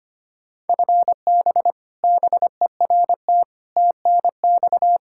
Morse code, heard as a keyed tone, about 25 wpm, 700 Hz. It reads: FB BERT TNX
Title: Practice QSO - Exchange 3 - More Info - 25wpm